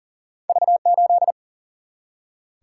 Morse code, heard as a keyed tone, about 40 wpm, 700 Hz. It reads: V8
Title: Single-Letter Single-Number 40wpm